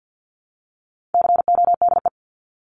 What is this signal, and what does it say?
Morse code, heard as a keyed tone, about 50 wpm, 700 Hz. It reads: CODE
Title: Top 1501-1750 Words 50wpm